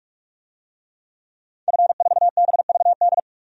Morse code, heard as a keyed tone, about 44 wpm, 690 Hz. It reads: F4BVD
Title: DXCC-100 Call Signs - 5 Characters - 45wpm